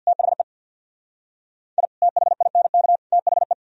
Morse code, heard as a keyed tone, about 62 wpm, 700 Hz. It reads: THE I THINK THE
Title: Speed-Racing - Sets of 3 Words 50wpm